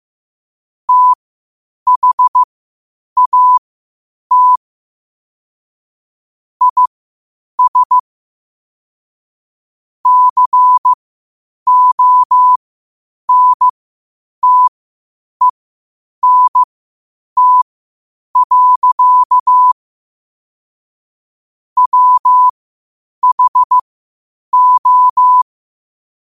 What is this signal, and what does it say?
Morse code, heard as a keyed tone, about 15 wpm, spaced as for 8 wpm, 1000 Hz. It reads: THAT IS CONTENT. WHO
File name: qotdfiles / 8wpm